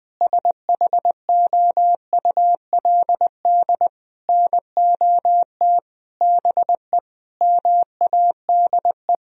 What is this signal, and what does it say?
Morse code, heard as a keyed tone, about 20 wpm, 700 Hz. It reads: SHOULD NOT BE MADE
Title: Sentences from Top 1000 Words - Review Part 3 - 20wpm